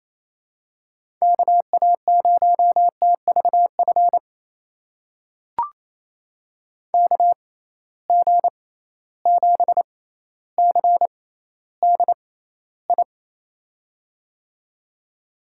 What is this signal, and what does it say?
Morse code, heard as a keyed tone, about 28 wpm, spaced as for 14 wpm, 700 Hz. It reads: KA0TVF KG7CDI
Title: USA Call Signs - 6 Characters - 6x Character Spacing - 28wpm